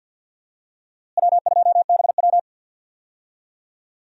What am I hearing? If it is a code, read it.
Morse code, heard as a keyed tone, about 50 wpm, 700 Hz. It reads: W2BW
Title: USA Call Signs - 4 Characters - Repeated 3x - 50wpm